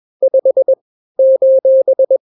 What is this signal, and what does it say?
Morse code, heard as a keyed tone, about 21 wpm, 530 Hz. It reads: 5 :
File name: daily / 20wpm